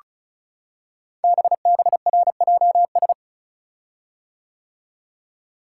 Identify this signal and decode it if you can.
Morse code, heard as a keyed tone, about 36 wpm, 700 Hz. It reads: BBRJS
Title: Code Groups - 5 Letters - 35wpm